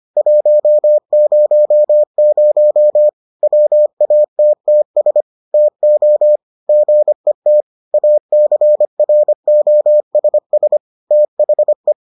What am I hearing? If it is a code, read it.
Morse code, heard as a keyed tone, about 25 wpm, 600 Hz. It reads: 100 WATTS TO GET ACROSS THE